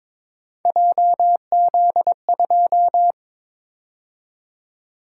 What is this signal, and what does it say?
Morse code, heard as a keyed tone, about 26 wpm, 700 Hz. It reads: JZ2
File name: Speed-Racing - Three Letter-Number Combinations - 22wpm